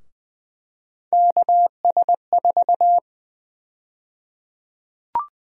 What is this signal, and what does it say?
Morse code, heard as a keyed tone, about 20 wpm, 700 Hz. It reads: KS4
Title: USA Call Sign Prefixes - Repeated 2x - 20wpm